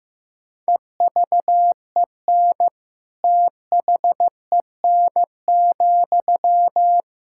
Morse code, heard as a keyed tone, about 15 wpm, 700 Hz. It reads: EVEN THEN,
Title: Sentences from Top 1600 Words 15wpm